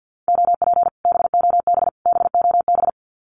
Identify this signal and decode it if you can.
Morse code, heard as a keyed tone, about 50 wpm, 700 Hz. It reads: OP BOB BOB
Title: Common QSO Phrases Encore 50wpm